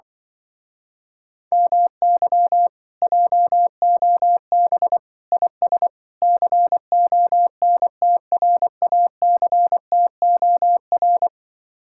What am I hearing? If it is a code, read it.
Morse code, heard as a keyed tone, about 24 wpm, 700 Hz. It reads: MY JOB IS CONTRACTOR
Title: Speed-Racing Common QSO Phrases - 20wpm